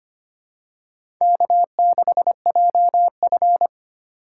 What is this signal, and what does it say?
Morse code, heard as a keyed tone, about 25 wpm, 700 Hz. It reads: K6JF
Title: USA Call Signs - 4 Characters - 2x Character Spacing - 25wpm